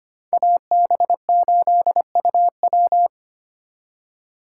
Morse code, heard as a keyed tone, about 32 wpm, 700 Hz. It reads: AB8UW